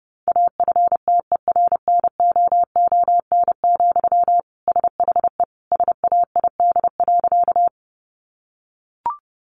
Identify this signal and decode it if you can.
Morse code, heard as a keyed tone, about 30 wpm, 700 Hz. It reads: AFTERNOON, SHE SAID.
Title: Sentences from Top 1500 Words 30wpm